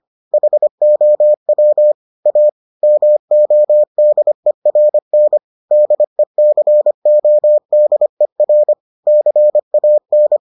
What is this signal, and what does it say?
Morse code, heard as a keyed tone, about 25 wpm, 600 Hz. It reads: HOW A MODERN DECODER CAN